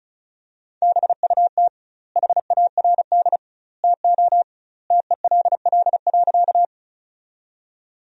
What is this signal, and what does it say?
Morse code, heard as a keyed tone, about 36 wpm, 700 Hz. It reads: BUT HARD TO TELL.
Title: 2x Word-Spacing - Sentences from Top 300 Words - 35wpm